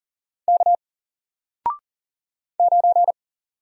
Morse code, heard as a keyed tone, about 40 wpm, 700 Hz. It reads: K 9
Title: Single Character (Letter:Number:Punctuation:Prosign) Alt 40wpm